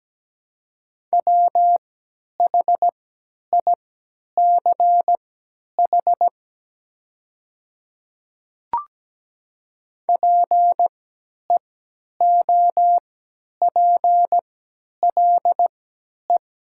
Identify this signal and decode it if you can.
Morse code, heard as a keyed tone, about 17 wpm, spaced as for 10 wpm, 700 Hz. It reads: WHICH PEOPLE
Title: 3x Character Spacing - Top 200 Words - 17wpm